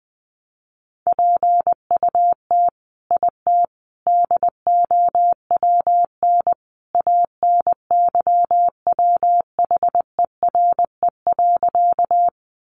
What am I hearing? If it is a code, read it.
Morse code, heard as a keyed tone, about 20 wpm, 700 Hz. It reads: PUT IT DOWN ANYWHERE.
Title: Sentences from Top 1400 Words 20wpm